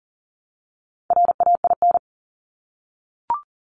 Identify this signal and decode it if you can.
Morse code, heard as a keyed tone, about 38 wpm, 690 Hz. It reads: RAIN
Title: Multi-letter Phonemes 40wpm